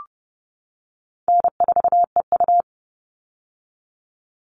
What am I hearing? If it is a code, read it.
Morse code, heard as a keyed tone, about 30 wpm, 700 Hz. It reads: N4EU
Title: All Cut Numbers - 4-Digit 30wpm